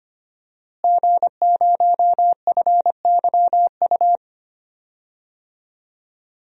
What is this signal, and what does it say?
Morse code, heard as a keyed tone, about 25 wpm, 700 Hz. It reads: G0FYU
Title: DXCC-100 Call Signs - 5 Characters - Repeated 2x - 25wpm